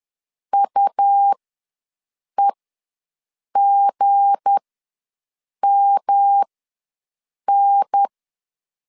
Morse code, heard as a keyed tone, about 10 wpm, spaced as for 6 wpm, 800 Hz. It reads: UEGMN
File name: LEZIONI / 7 GF9Z / 7 G ws 10wpm 3843